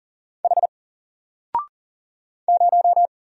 Morse code, heard as a keyed tone, about 40 wpm, 700 Hz. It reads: H 0
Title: Single Letter-Number 40wpm